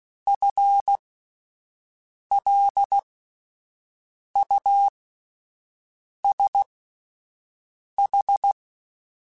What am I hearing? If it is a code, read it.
Morse code, heard as a keyed tone, about 16 wpm, spaced as for 5 wpm, 800 Hz. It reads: FLUSH
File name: Spaced / FL Runs sp mp3s / FL Runs sp @10